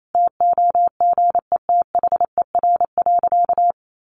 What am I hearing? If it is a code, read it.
Morse code, heard as a keyed tone, about 28 wpm, 700 Hz. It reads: TOGETHER.
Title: Sentences from Top 1500 Words 28wpm